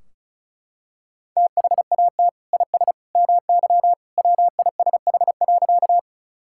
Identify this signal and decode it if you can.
Morse code, heard as a keyed tone, about 36 wpm, 700 Hz. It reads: THAT IS MY WISH.
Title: Sentences from Top 1600 Words 35wpm